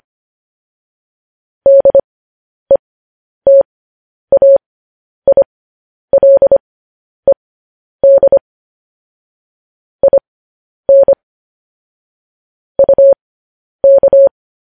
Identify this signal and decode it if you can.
Morse code, heard as a keyed tone, about 25 wpm, spaced as for 10 wpm, 550 Hz. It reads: DETAILED IN UK
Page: news headlines in morse code at 10 wpm